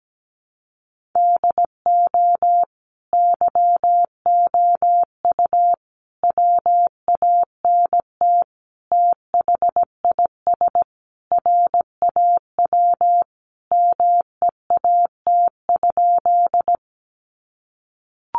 Morse code, heard as a keyed tone, about 17 wpm, 700 Hz. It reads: DO YOU WANT THIS RAW MEAT?
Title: Sentences from Top 1500 Words 17wpm